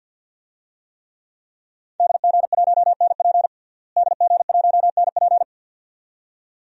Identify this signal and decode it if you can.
Morse code, heard as a keyed tone, about 53 wpm, 700 Hz. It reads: DG1NP DG1NP
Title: DXCC-100 Call Signs - 5 Characters - Repeated 2x - 50wpm